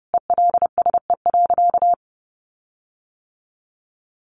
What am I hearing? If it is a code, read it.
Morse code, heard as a keyed tone, about 30 wpm, 700 Hz. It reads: ELSE.
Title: Sentences from Top 1500 Words 30wpm